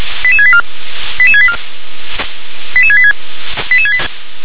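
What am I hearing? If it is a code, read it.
Five-tone selective call (ZVEI), heard as quick runs of five tones, tones of about 70 ms. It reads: 88684 88684 88686
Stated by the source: ZVEI five-tone selective call received via SDR.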